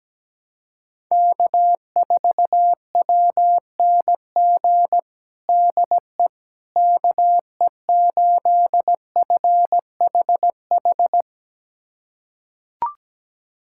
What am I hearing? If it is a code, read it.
Morse code, heard as a keyed tone, about 17 wpm, 700 Hz. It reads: K4WNG DE KE8FHH
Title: USA Callsign DE Callsign 17wpm